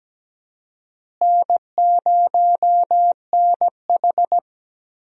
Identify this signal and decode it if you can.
Morse code, heard as a keyed tone, about 17 wpm, 700 Hz. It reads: N0NH
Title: USA Call Signs - 4 Characters - Repeated 2x - 17wpm